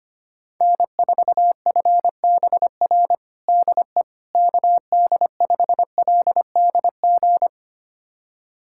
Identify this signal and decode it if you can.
Morse code, heard as a keyed tone, about 25 wpm, 700 Hz. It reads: N4FBR DE KD5LDG
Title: USA Callsign DE Callsign 25wpm